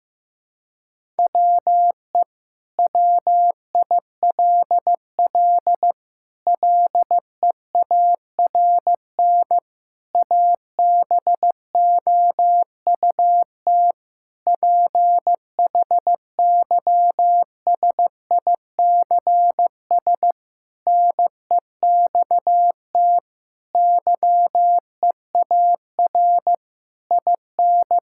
Morse code, heard as a keyed tone, about 15 wpm, 700 Hz. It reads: WE WILL LEARN ABOUT PHYSICS NEXT YEAR IN
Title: Sentences from Top 1000 Words - Review Part 5 - 15wpm